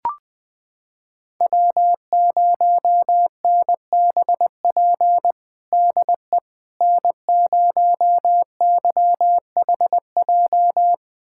Morse code, heard as a keyed tone, about 21 wpm, 700 Hz. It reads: W0NBP DE N0YHJ
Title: USA Callsign DE Callsign - 5 Characters - 20wpm